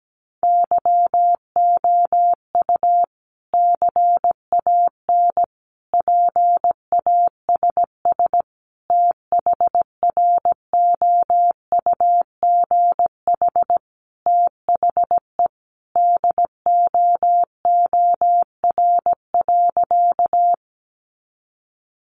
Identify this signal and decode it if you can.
Morse code, heard as a keyed tone, about 17 wpm, 700 Hz. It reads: YOU CAN PASS THROUGH THE DOOR.
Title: Sentences from Top 400 Words 17wpm